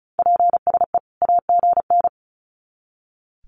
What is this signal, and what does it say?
Morse code, heard as a keyed tone, about 34 wpm, 700 Hz. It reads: PSE AGN
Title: Common QSO Phrases 35wpm